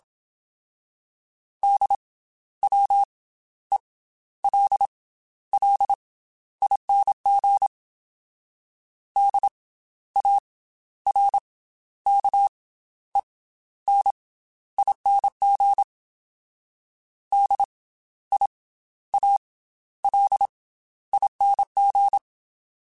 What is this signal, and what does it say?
Morse code, heard as a keyed tone, about 27 wpm, spaced as for 13 wpm, 800 Hz. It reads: DWELLING DARKENING DIALING
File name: Special INGs / Special INGs-mp3s / Slow ing @25